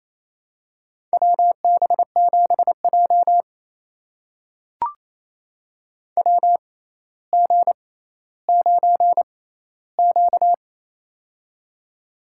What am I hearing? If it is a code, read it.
Morse code, heard as a keyed tone, about 28 wpm, spaced as for 14 wpm, 700 Hz. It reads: WB7J WG9Q
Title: USA Call Signs - 4 Characters - 6x Character Spacing - 28wpm